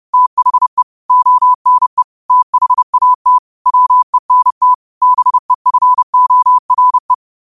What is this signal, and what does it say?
Morse code, heard as a keyed tone, about 30 wpm, 1000 Hz. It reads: THE ONE THAT WENT BEFORE
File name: qotdfiles / 30wpm